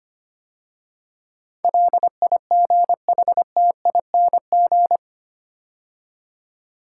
Morse code, heard as a keyed tone, about 25 wpm, 700 Hz. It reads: LIGHTING
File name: 8 Letter Words - First 3 Letters Given - 25wpm